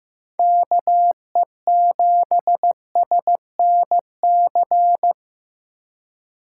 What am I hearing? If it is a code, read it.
Morse code, heard as a keyed tone, about 15 wpm, 700 Hz. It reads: KE7SNC
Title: USA Call Signs - 4x Character Spacing - 15wpm